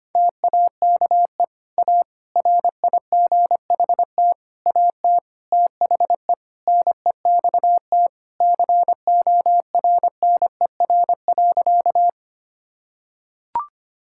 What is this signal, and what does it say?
Morse code, heard as a keyed tone, about 25 wpm, 700 Hz. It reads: TAKE A RIGHT AT THE NEXT CORNER.
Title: Sentences from Top 1600 Words 25wpm